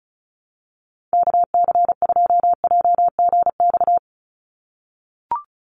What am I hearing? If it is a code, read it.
Morse code, heard as a keyed tone, about 36 wpm, 700 Hz. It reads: KC2JGX
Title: USA Call Signs 35wpm